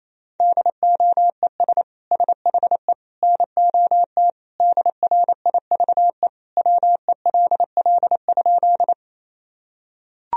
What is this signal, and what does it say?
Morse code, heard as a keyed tone, about 28 wpm, 700 Hz. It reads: DOES SHE NOT DRIVE WELL?
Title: Sentences from Top 1000 Words - Review Part 2 - 28wpm